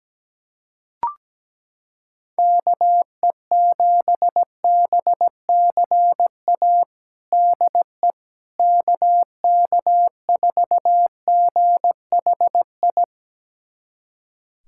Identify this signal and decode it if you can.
Morse code, heard as a keyed tone, about 17 wpm, 700 Hz. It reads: KE7BCA DE KK4GHI
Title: USA Callsign DE Callsign 17wpm